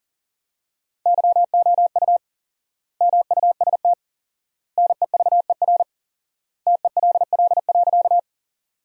Morse code, heard as a keyed tone, about 40 wpm, 700 Hz. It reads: YOU MUST NEVER TELL.
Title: 4x Word-Spacing - Sentences from Top 300 Words - 40wpm